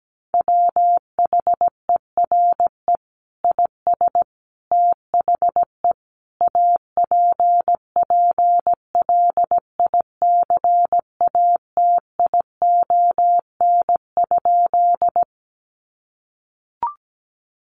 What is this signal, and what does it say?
Morse code, heard as a keyed tone, about 17 wpm, 700 Hz. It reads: WHERE IS THE APPLICATION?
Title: Sentences from Top 1200 Words 17wpm